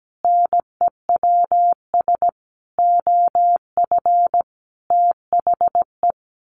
Morse code, heard as a keyed tone, about 17 wpm, 700 Hz. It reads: NEWS OF THE
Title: Sentences from Top 1500 Words 17wpm